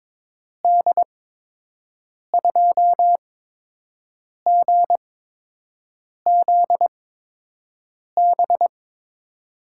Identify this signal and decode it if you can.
Morse code, heard as a keyed tone, about 22 wpm, spaced as for 6 wpm, 700 Hz. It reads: D2GZB
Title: USA Call Signs - 8x Character Spacing - 22wpm